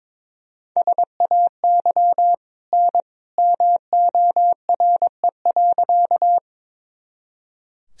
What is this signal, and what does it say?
Morse code, heard as a keyed tone, about 22 wpm, 700 Hz. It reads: SAY N MORE.
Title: Sentences from Top 200 Words and Abbreviations 22wpm